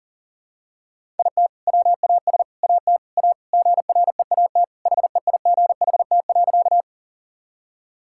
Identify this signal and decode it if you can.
Morse code, heard as a keyed tone, about 37 wpm, 700 Hz. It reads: IT WAS AT A GREAT HEIGHT.
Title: Sentences from Top 1600 Words 40wpm